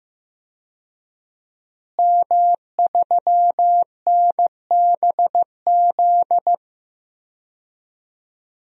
Morse code, heard as a keyed tone, about 15 wpm, 700 Hz. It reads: M3NBZ
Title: DXCC-100 Call Signs - 5 Characters - Repeated 2x - 15wpm